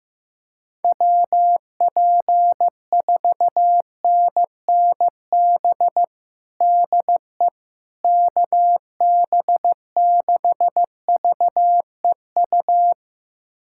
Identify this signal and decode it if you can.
Morse code, heard as a keyed tone, about 15 wpm, 700 Hz. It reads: WP4NNB DE KB6VEU
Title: USA Callsign DE Callsign - 6 Characters - 15wpm